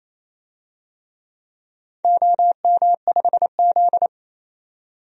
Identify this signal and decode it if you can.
Morse code, heard as a keyed tone, about 28 wpm, 700 Hz. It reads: OM5Z
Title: DXCC-100 Call Signs - 4 Characters - 28wpm